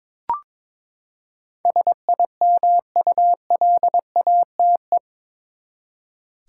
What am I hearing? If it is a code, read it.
Morse code, heard as a keyed tone, about 22 wpm, 700 Hz. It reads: SIMULATE
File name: practice / 22wpm / Eight Letter Words - 22wpm